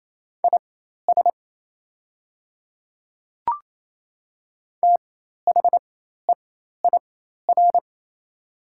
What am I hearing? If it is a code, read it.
Morse code, heard as a keyed tone, about 28 wpm, spaced as for 13 wpm, 700 Hz. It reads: IS THEIR
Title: 4x Character Spacing - Top 100 Words - 28wpm